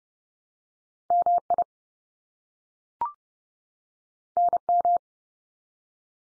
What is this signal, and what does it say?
Morse code, heard as a keyed tone, about 30 wpm, 700 Hz. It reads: MI NM
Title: US States 30WPM